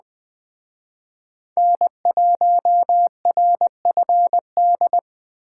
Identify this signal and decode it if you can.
Morse code, heard as a keyed tone, about 20 wpm, 700 Hz. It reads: N1RFD
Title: Speed Racing - USA Call Signs 17wpm